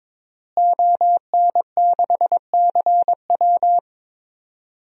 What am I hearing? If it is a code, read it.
Morse code, heard as a keyed tone, about 21 wpm, 700 Hz. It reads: ON6CW
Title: DXCC-100 Call Signs - 5 Characters - Repeated 3x - 22wpm